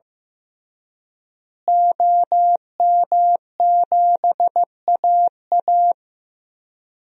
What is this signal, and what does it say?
Morse code, heard as a keyed tone, about 15 wpm, 700 Hz. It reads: OM7AA
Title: DXCC-100 Call Signs - 5 Characters - Repeated 3x - 15wpm